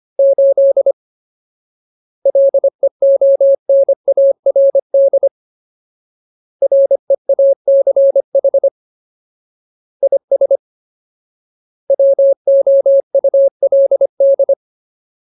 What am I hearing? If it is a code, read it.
Morse code, heard as a keyed tone, about 25 wpm, 550 Hz. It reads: 8 LEONARD REACH IS WOULD
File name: Ultimate Word List / 25WPM Ultimate